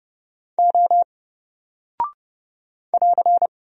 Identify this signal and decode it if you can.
Morse code, heard as a keyed tone, about 30 wpm, 700 Hz. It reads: O <AR>
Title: Single Character (Letter:Number:Punctuation:Prosign) 30wpm